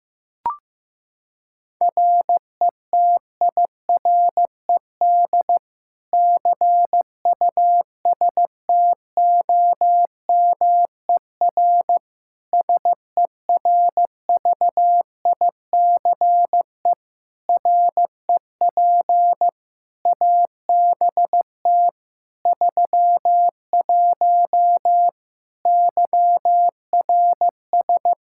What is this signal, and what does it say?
Morse code, heard as a keyed tone, about 15 wpm, 700 Hz. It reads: RETIRED CUSTOMER SERVICE REP ABT 31 YRS
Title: Practice QSO - Exchange 3 - More Info - 15wpm